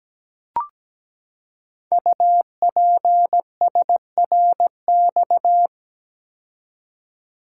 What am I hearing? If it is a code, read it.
Morse code, heard as a keyed tone, about 17 wpm, 700 Hz. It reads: UPSRX
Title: Code Groups - 5 Letters - 17wpm